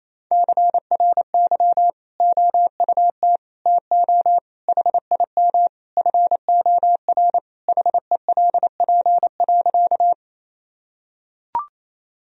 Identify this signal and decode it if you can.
Morse code, heard as a keyed tone, about 28 wpm, 700 Hz. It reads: CRY OUT TO HIM FOR HELP.
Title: Sentences from Top 1000 Words - Review Part 2 - 28wpm